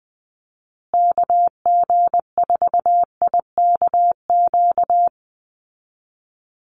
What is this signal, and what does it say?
Morse code, heard as a keyed tone, about 20 wpm, 700 Hz. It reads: KG4IKQ
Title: Call Signs Repeated 2x - 20wpm